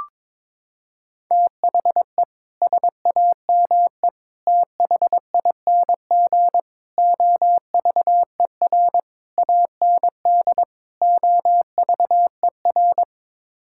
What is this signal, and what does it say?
Morse code, heard as a keyed tone, about 22 wpm, 700 Hz. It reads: THE SAME THING OVER AND OVER
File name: practice / 22wpm / Sets of 6 Words - 22wpm